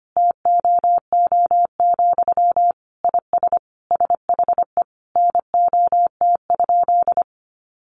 Morse code, heard as a keyed tone, about 25 wpm, 700 Hz. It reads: TOO, IS SHE NOT?
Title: Sentences from Top 1300 Words 25wpm